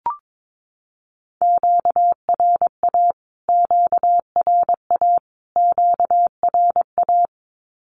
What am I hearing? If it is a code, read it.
Morse code, heard as a keyed tone, about 22 wpm, 700 Hz. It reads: QRA QRA QRA
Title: Q-Codes Repeated 3x 22wpm